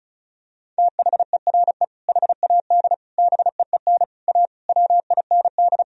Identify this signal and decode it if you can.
Morse code, heard as a keyed tone, about 36 wpm, 700 Hz. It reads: THERE HAD BEEN A WIND
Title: Sentences from Top 1000 Words - Review Part 2 - 35wpm